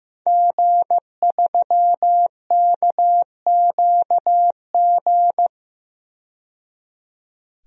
Morse code, heard as a keyed tone, about 15 wpm, 700 Hz. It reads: G3KQG
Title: DXCC-100 Call Signs - 5 Characters - 15wpm